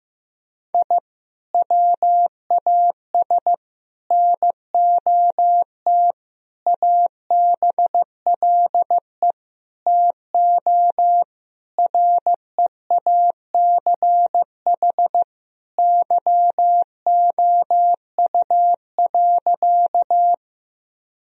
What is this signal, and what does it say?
Morse code, heard as a keyed tone, about 16 wpm, 700 Hz. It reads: I WAS NOT ABLE TO REACH YOU.
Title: Sentences from Top 1000 Words - Review Part 2 - 15wpm